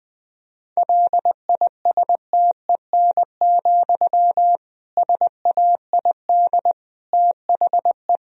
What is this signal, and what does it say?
Morse code, heard as a keyed tone, about 21 wpm, 700 Hz. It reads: LISTEN, SAID THE
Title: Sentences from Top 1000 Words - Review Part 3 - 20wpm